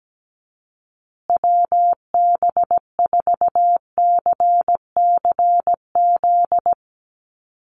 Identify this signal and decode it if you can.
Morse code, heard as a keyed tone, about 17 wpm, 700 Hz. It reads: WB4CCZ
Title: USA Call Signs Repeated 2x 17wpm